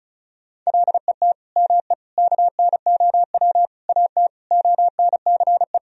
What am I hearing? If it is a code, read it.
Morse code, heard as a keyed tone, about 36 wpm, 700 Hz. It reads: LET ME KNOW AT ONCE
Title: Sentences from Top 300 Words and Abbreviations 35wpm